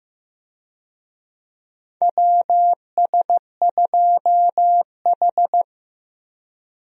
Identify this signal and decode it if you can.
Morse code, heard as a keyed tone, about 15 wpm, 700 Hz. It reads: WS2H
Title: USA Call Signs - 4 Characters - 15wpm